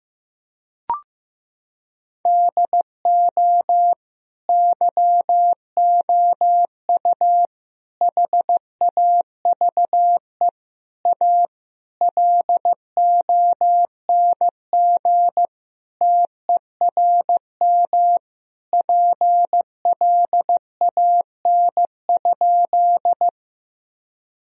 Morse code, heard as a keyed tone, about 16 wpm, 700 Hz. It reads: DO YOU HAVE A LONG TERM PLAN?
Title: Sentences from Top 1000 Words - Review Part 4 - 15wpm